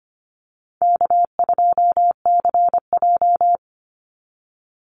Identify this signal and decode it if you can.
Morse code, heard as a keyed tone, about 25 wpm, 700 Hz. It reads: K2CJ
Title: Easy Call Signs Repeated 2x - 25wpm